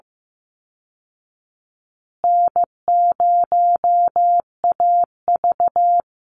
Morse code, heard as a keyed tone, about 15 wpm, 700 Hz. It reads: N0AV
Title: Easy Call Signs Repeated 2x - 15wpm